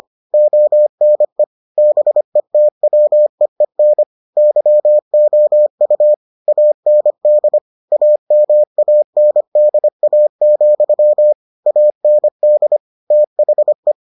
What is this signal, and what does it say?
Morse code, heard as a keyed tone, about 25 wpm, 600 Hz. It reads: ONE BETWEEN YOU AND AMANDA, AND THE